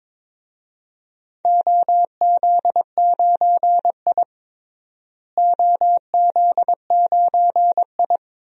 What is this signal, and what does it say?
Morse code, heard as a keyed tone, about 21 wpm, 700 Hz. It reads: OZ9I OZ9I
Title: DXCC-100 Call Signs - 4 Characters - Repeated 2x - 22wpm